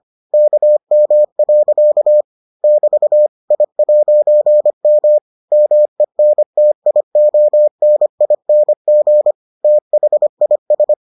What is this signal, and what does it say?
Morse code, heard as a keyed tone, about 25 wpm, 600 Hz. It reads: KM. <BT> I'M MENTIONING THIS